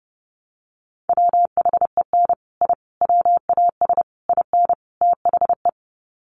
Morse code, heard as a keyed tone, about 28 wpm, 700 Hz. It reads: WHEN I WAS IN THE
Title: Sets of 5 Words Encore2 30wpm